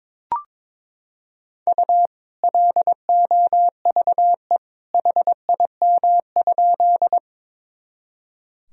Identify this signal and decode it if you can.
Morse code, heard as a keyed tone, about 22 wpm, 700 Hz. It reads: U LOVE HIM?
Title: Sentences from Top 400 Words and Abbreviations 22wpm